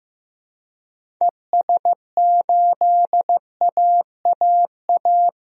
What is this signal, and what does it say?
Morse code, heard as a keyed tone, about 15 wpm, 700 Hz. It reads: ES8AAA
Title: DXCC-100 Call Signs - 6 Characters - 15wpm